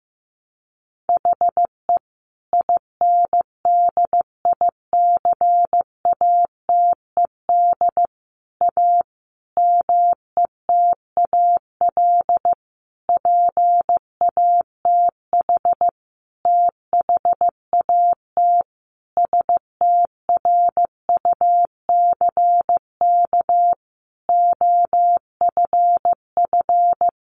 Morse code, heard as a keyed tone, about 15 wpm, 700 Hz. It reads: HE INDICATED A METAL PATH THAT STRUCK OFF
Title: A Sound Of Thunder 15wpm